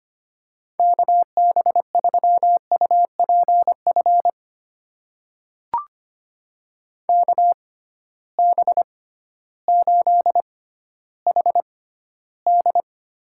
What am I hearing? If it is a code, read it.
Morse code, heard as a keyed tone, about 25 wpm, spaced as for 14 wpm, 700 Hz. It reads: KB3UPF KB8HD
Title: USA Call Signs - 6 Characters - 6x Character Spacing - 25wpm